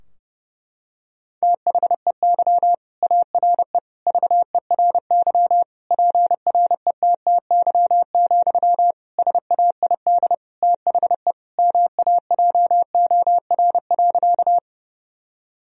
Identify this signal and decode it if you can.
Morse code, heard as a keyed tone, about 30 wpm, 700 Hz. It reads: THEY ARE VERY PRETTY, SAID THE MAJOR.
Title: Sentences from Top 1000 Words - Review Part 4 - 30wpm